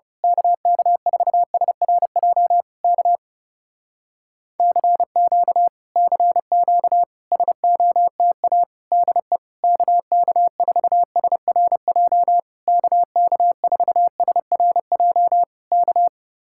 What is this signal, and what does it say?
Morse code, heard as a keyed tone, about 29 wpm, 700 Hz. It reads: KK4SRJ K CQ CQ SOTA DE KK4SRJ KK4SRJ K
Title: Speed-Racing Common QSO Phrases - 30wpm